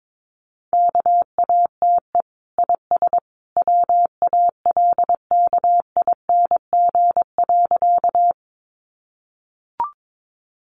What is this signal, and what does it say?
Morse code, heard as a keyed tone, about 22 wpm, 700 Hz. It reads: KATE IS WALKING.
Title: Easy Phrases 22wpm